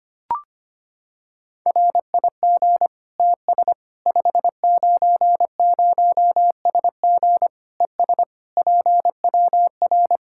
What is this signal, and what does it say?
Morse code, heard as a keyed tone, about 25 wpm, 700 Hz. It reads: RIG TS 590SG ES PWR
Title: Practice QSO - Exchange 2 - Info - 25wpm